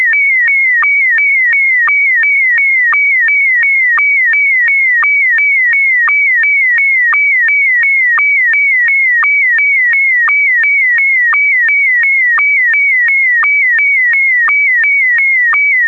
SSTV (slow-scan television), Scottie DX.
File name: sstv / sstv1 / scottie dx